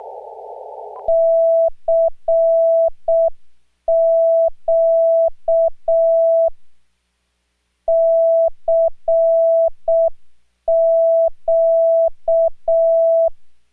Morse code, heard as a keyed tone, about 6 wpm, 660 Hz. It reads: CQ CQ